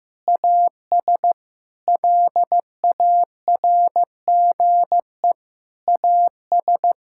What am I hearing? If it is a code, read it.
Morse code, heard as a keyed tone, about 16 wpm, 700 Hz. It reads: AS LARGE AS
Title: Sentences from Top 1000 Words - Review Part 6 - 15wpm